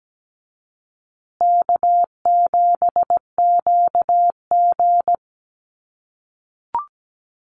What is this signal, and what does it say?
Morse code, heard as a keyed tone, about 17 wpm, 700 Hz. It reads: K7QG
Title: USA Call Signs Repeated 3x 17wpm